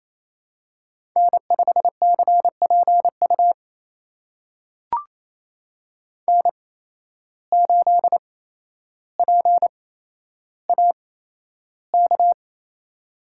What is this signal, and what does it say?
Morse code, heard as a keyed tone, about 28 wpm, spaced as for 12 wpm, 700 Hz. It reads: N5CPU N8PAK
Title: USA Call Signs - 5 Characters - 8x Character Spacing - 28wpm